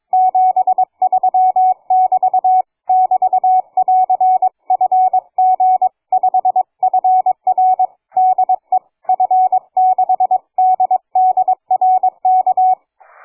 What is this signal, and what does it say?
Morse code, heard as a keyed tone, about 22 wpm, 740 Hz. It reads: 73<BT> <BT><AR>FG5FR DE F6DDR K
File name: cw forum / fg5fr 160m 02h30